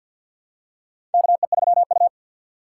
Morse code, heard as a keyed tone, about 49 wpm, 700 Hz. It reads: KE3U
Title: USA Call Signs - 4 Characters - 3x Character Spacing - 50wpm